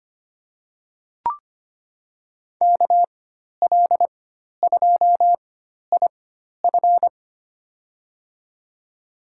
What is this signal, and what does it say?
Morse code, heard as a keyed tone, about 25 wpm, spaced as for 12 wpm, 700 Hz. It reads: KL2IF
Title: USA Call Signs - 5 Characters - 4x Character Spacing - 25wpm